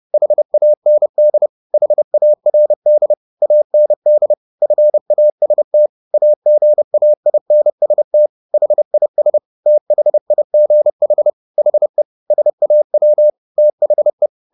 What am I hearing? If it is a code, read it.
Morse code, heard as a keyed tone, about 30 wpm, 600 Hz. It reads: HAND HARD AND FAST AGAINST HIS THIGH HE SAW THE
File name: the old man and the sea / the old man and the sea 1 30wpm